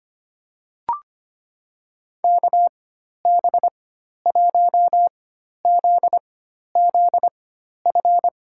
Morse code, heard as a keyed tone, about 25 wpm, spaced as for 12 wpm, 700 Hz. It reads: KB1ZZF
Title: USA Call Signs - 6 Characters - 4x Character Spacing - 25wpm